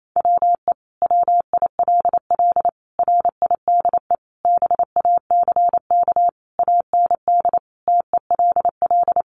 Morse code, heard as a keyed tone, about 28 wpm, 700 Hz. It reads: WE WILL RIDE BACK AND TELL
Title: Sentences from Top 1200 Words 28wpm